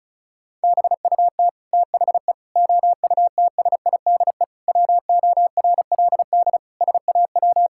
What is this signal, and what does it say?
Morse code, heard as a keyed tone, about 36 wpm, 700 Hz. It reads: BUT THE OUTSIDE WORLD SAW
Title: Sentences from Top 1000 Words - Review Part 6 - 35wpm